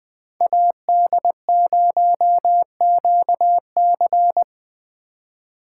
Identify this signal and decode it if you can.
Morse code, heard as a keyed tone, about 23 wpm, 700 Hz. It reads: AD0QC
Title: Speed Racing - USA 2x2 Call Signs 20wpm